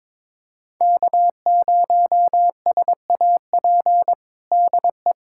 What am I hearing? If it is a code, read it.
Morse code, heard as a keyed tone, about 22 wpm, 700 Hz. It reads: K0SAP DE
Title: USA Callsign DE Callsign - 5 Characters - 22wpm